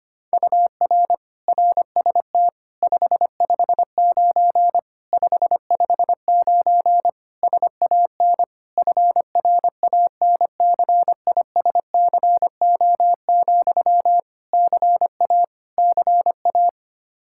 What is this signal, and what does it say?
Morse code, heard as a keyed tone, about 25 wpm, 700 Hz. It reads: UR RST 559 559 SAN FRANCISCO, CA CA
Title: Practice QSO - Exchange 1 - Beginning - 25wpm